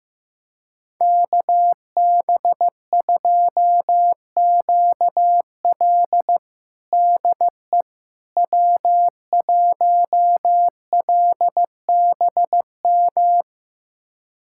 Morse code, heard as a keyed tone, about 15 wpm, 700 Hz. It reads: KB2QL DE W1LBM
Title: USA Callsign DE Callsign - 5 Characters - 15wpm